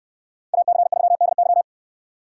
Morse code, heard as a keyed tone, about 64 wpm, 700 Hz. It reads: AC3DY